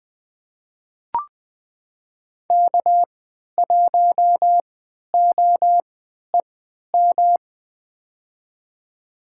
Morse code, heard as a keyed tone, about 21 wpm, spaced as for 12 wpm, 700 Hz. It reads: K1OEM
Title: USA Call Signs - 5 Characters - 3x Character Spacing - 20wpm